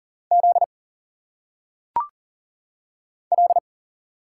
Morse code, heard as a keyed tone, about 38 wpm, 700 Hz. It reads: Z L
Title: Single Letters - Phonetic Pronunciation 40wpm